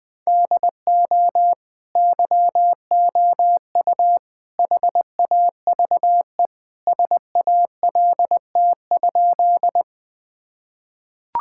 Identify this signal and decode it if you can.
Morse code, heard as a keyed tone, about 21 wpm, 700 Hz. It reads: DO YOU HAVE SALT?
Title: Sentences from Top 1000 Words - Review Part 5 - 20wpm